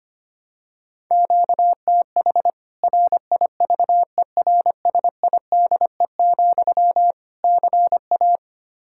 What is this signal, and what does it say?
Morse code, heard as a keyed tone, about 25 wpm, 700 Hz. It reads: QTH RIVERSIDE, CA
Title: Practice QSO - Exchange 1 - Beginning - 25wpm